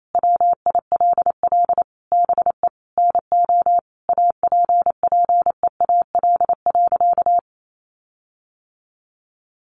Morse code, heard as a keyed tone, about 28 wpm, 700 Hz. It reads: WILL BE NO APPEAL.
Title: Sentences from Top 1300 Words 28wpm